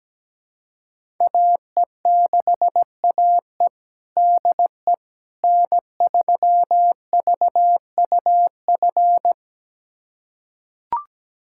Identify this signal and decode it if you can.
Morse code, heard as a keyed tone, about 17 wpm, 700 Hz. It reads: AE6AE DE N3VUF
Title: USA Callsign DE Callsign - 5 Characters - 17wpm